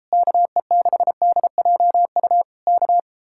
Morse code, heard as a keyed tone, about 33 wpm, 700 Hz. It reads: KE6DJU K